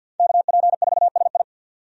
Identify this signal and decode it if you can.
Morse code, heard as a keyed tone, about 46 wpm, 700 Hz. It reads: KP4SI
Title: USA Call Signs - 5 Characters - 50wpm